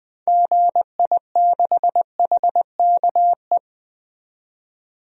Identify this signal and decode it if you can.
Morse code, heard as a keyed tone, about 20 wpm, 700 Hz. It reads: GI6HKE
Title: DXCC-100 Call Signs - 6 Characters - 20wpm